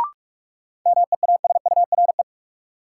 Morse code, heard as a keyed tone, about 46 wpm, 700 Hz. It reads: MEASURE
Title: Sentences from Top 1200 Words - New Word Drill Alt - 45wpm